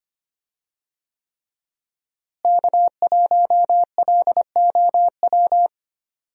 Morse code, heard as a keyed tone, about 25 wpm, 700 Hz. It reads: K1LOW
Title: USA Call Signs - 5 Characters - Repeated 3x - 25wpm